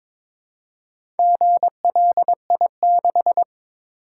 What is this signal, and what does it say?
Morse code, heard as a keyed tone, about 22 wpm, 700 Hz. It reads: GLI6
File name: practice / 22wpm / Four Letter-Number Combinations - 22wpm